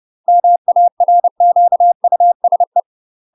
Morse code, heard as a keyed tone, about 30 wpm, 700 Hz. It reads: MARQUSE
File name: UK MASTER QSOs TRAINING / 30WPM / UK MASTER QSOs 0040